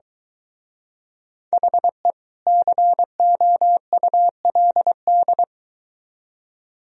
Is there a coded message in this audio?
Yes: Morse code from 1.5 to 5.4 s, heard as a keyed tone, about 23 wpm, 700 Hz. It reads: HE COULD